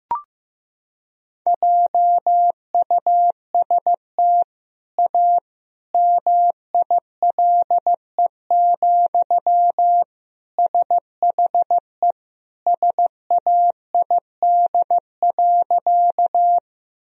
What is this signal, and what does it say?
Morse code, heard as a keyed tone, about 15 wpm, 700 Hz. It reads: JUST A MILE, SHE SAID.
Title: Sentences from Top 1000 Words - Review Part 1 - 15wpm